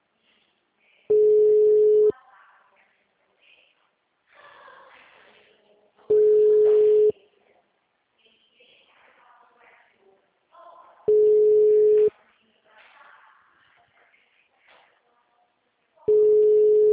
Ringback tone (Europe, 425 Hz).